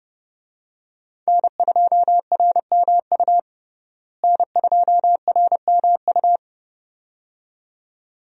Morse code, heard as a keyed tone, about 30 wpm, 700 Hz. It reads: N2RMU N2RMU
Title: USA Call Signs - 5 Characters - Repeated 2x - 30wpm